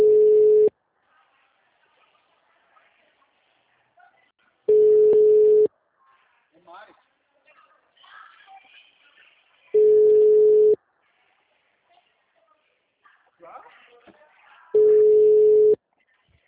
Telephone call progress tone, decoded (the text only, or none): ringback tone